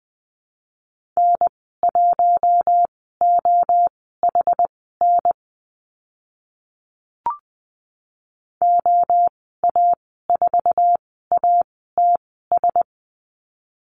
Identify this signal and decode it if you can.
Morse code, heard as a keyed tone, about 20 wpm, spaced as for 14 wpm, 700 Hz. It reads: N1OHN OA4ATS
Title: 062 Lesson 11g - Letter h - Call Signs 20wpm